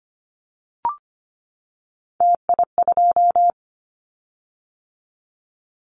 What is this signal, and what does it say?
Morse code, heard as a keyed tone, about 25 wpm, 700 Hz. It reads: TI2
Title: Callsign Prefixes for Top 100 DXCC - 25wpm